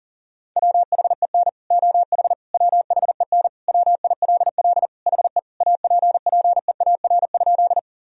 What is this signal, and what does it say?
Morse code, heard as a keyed tone, about 37 wpm, 700 Hz. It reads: WHEN OH WHEN WILL HE APPEAR?
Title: Sentences from Top 1000 Words - Review Part 2 - 40wpm